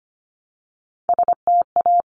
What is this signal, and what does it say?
Morse code, heard as a keyed tone, about 25 wpm, 700 Hz. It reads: STA
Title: Top Three Letter Combinations in Words 25wpm